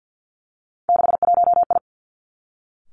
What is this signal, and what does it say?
Morse code, heard as a keyed tone, about 50 wpm, 700 Hz. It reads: 61I
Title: Three Challenging Letter-Number Combinations 50wpm